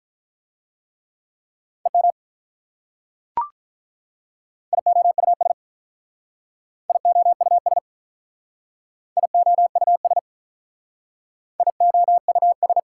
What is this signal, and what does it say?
Morse code, heard as a keyed tone, about 44 wpm, 690 Hz. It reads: EM IOUS IOUS IOUS IOUS